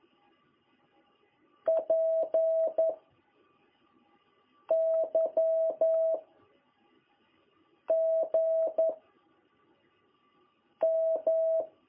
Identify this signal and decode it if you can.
Morse code, heard as a keyed tone, about 11 wpm, 660 Hz. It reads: P Y G M
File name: jh8cbh / 2021cw